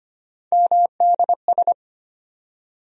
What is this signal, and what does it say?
Morse code, heard as a keyed tone, about 25 wpm, 700 Hz. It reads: MDS
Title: USA Call Sign Postfixes - 3 Letters 25wpm